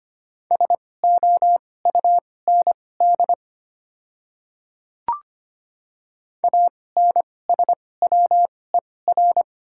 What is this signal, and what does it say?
Morse code, heard as a keyed tone, about 25 wpm, spaced as for 18 wpm, 700 Hz. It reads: SOUND ANSWER
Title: 2x Character Spacing - Top 500 Words - 25wpm